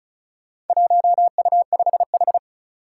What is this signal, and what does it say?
Morse code, heard as a keyed tone, about 34 wpm, 700 Hz. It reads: 1U5H
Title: Four Challenging Letter-Number Combinations 35wpm